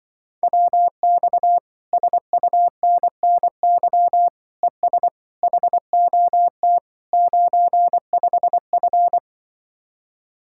Morse code, heard as a keyed tone, about 23 wpm, 700 Hz. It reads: WX SUNNY ES HOT 95F
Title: Speed-Racing Common QSO Phrases - 20wpm